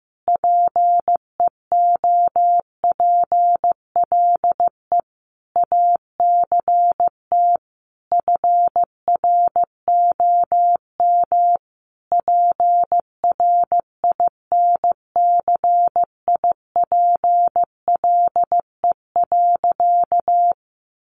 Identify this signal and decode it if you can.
Morse code, heard as a keyed tone, about 15 wpm, 700 Hz. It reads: PEOPLE ACT FROM PRINCIPLE.
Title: Sentences from Top 1400 Words 15wpm